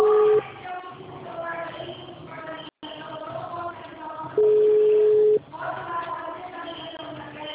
Ringback tone (Europe, 425 Hz).